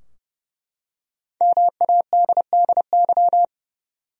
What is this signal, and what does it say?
Morse code, heard as a keyed tone, about 30 wpm, 700 Hz. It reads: MADDY
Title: USA Common Names - Max 5 Letters - 30wpm